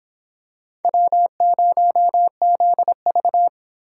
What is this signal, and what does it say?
Morse code, heard as a keyed tone, about 25 wpm, 700 Hz. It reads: W0ZV